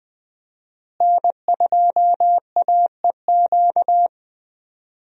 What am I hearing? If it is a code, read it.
Morse code, heard as a keyed tone, about 20 wpm, 700 Hz. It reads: N2AEQ
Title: USA Call Signs - 5 Characters - Repeated 3x - 20wpm